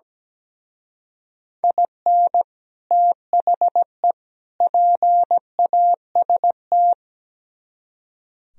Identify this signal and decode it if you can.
Morse code, heard as a keyed tone, about 19 wpm, 700 Hz. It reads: IN THE PAST
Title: Speed-Racing - Sets of 3 Words 17wpm